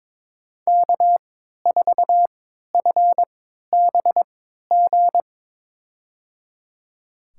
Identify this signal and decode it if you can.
Morse code, heard as a keyed tone, about 22 wpm, spaced as for 12 wpm, 700 Hz. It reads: K4FBG
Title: USA Call Signs - 5 Characters - 3x Character Spacing - 22wpm